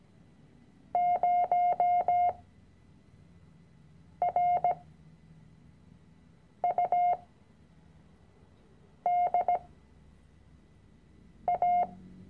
Morse code, heard as a keyed tone, about 17 wpm, spaced as for 4 wpm, 710 Hz. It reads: 0RUDA
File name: CW-Online-Kurs / Anfänger / Lektion 006